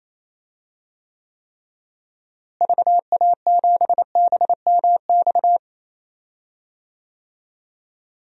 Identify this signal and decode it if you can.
Morse code, heard as a keyed tone, about 28 wpm, 700 Hz. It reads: VA7BMX
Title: DXCC-100 Call Signs - 6 Characters - Repeated 3x - 28wpm